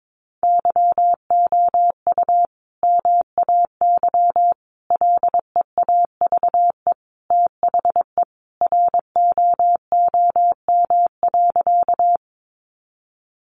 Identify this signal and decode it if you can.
Morse code, heard as a keyed tone, about 22 wpm, 700 Hz. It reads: YOU MAY LEAVE THE ROOM.
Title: Sentences from Top 400 Words 22wpm